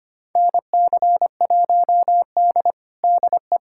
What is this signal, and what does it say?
Morse code, heard as a keyed tone, about 25 wpm, 700 Hz. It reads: NC1D DE
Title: USA Callsign DE Callsign - 4 Characters - 25wpm